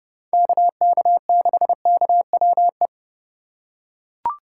Morse code, heard as a keyed tone, about 28 wpm, 700 Hz. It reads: KK6KWE